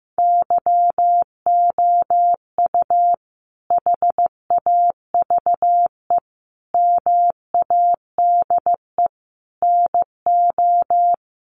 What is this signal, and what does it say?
Morse code, heard as a keyed tone, about 15 wpm, 700 Hz. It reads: YOU HAVE MADE NO
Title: Sentences from Top 1400 Words 15wpm